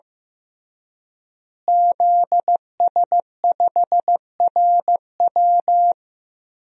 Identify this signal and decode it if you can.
Morse code, heard as a keyed tone, about 15 wpm, 700 Hz. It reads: ZS5RW
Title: DXCC-100 Call Signs - 5 Characters - Repeated 3x - 15wpm